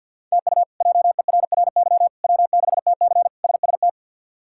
Morse code, heard as a keyed tone, about 51 wpm, 700 Hz. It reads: TU JERRY W6TC SST
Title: K1USN SST QSO Practice - 50wpm